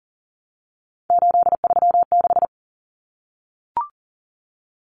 Morse code, heard as a keyed tone, about 40 wpm, 700 Hz. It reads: 836
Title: Numbers - 3-Digit 40wpm